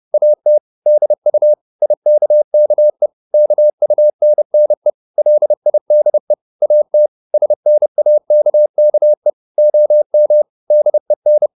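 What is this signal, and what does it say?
Morse code, heard as a keyed tone, about 30 wpm, 600 Hz. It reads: AT DU IKKE KUNNE LIDE AT SNAKKE OM DEN
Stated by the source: High-speed Morse (QRQ) at 30 wpm.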